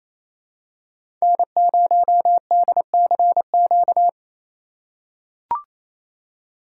Morse code, heard as a keyed tone, about 28 wpm, 700 Hz. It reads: N0DCQ
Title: USA Call Signs - 3x Character Spacing - 28wpm